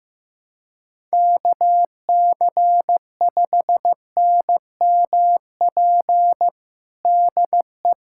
Morse code, heard as a keyed tone, about 15 wpm, 700 Hz. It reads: KC5NMP DE
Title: USA Callsign DE Callsign 15wpm